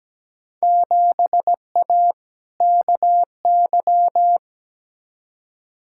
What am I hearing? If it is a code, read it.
Morse code, heard as a keyed tone, about 17 wpm, 700 Hz. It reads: 7A KY
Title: Field Day Exchanges Encore - 17wpm